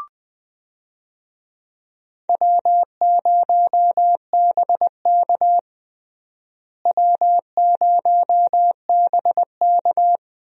Morse code, heard as a keyed tone, about 20 wpm, 700 Hz. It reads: W0BK W0BK
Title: USA Call Signs - 4 Characters - Repeated 2x - 20wpm